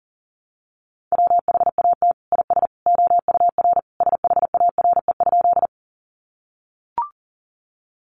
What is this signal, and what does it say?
Morse code, heard as a keyed tone, about 40 wpm, 700 Hz. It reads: WHAT IS OUR SHARE?
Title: Sentences from Top 900 Words 40wpm